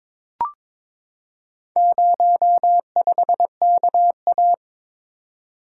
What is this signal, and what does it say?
Morse code, heard as a keyed tone, about 22 wpm, 700 Hz. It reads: 05KA
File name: practice / 22wpm / Four Letter-Number Combinations - 22wpm